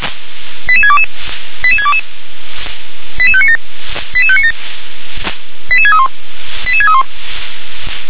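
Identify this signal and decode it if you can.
Five-tone selective call (ZVEI), heard as quick runs of five tones, tones of about 70 ms. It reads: 88622 88622 88597 88597 88631 88631
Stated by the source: ZVEI five-tone selective call received on an SDR.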